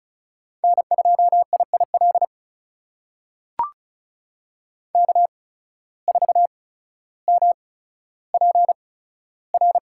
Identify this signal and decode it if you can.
Morse code, heard as a keyed tone, about 36 wpm, spaced as for 15 wpm, 700 Hz. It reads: N2IIL K4MPR
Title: USA Call Signs - 5 Characters - 8x Character Spacing - 35wpm